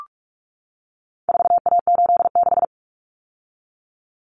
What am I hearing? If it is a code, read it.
Morse code, heard as a keyed tone, about 44 wpm, 700 Hz. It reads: 4A86
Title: Common Cut Numbers - 4-Digit 45wpm